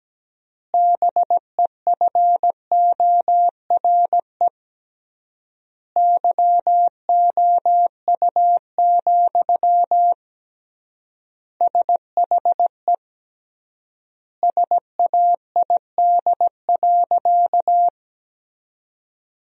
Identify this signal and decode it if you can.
Morse code, heard as a keyed tone, about 17 wpm, 700 Hz. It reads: BEFORE YOU, SHE SAID.
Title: 3x Word-Spacing - Sentences from Top 400 Words - 17wpm